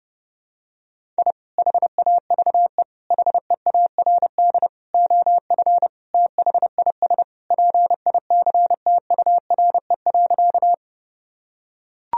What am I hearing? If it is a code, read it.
Morse code, heard as a keyed tone, about 29 wpm, 700 Hz. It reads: I HAVE HEARD OF THIS PICTURE.
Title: Sentences from Top 1000 Words - Review Part 1 - 30wpm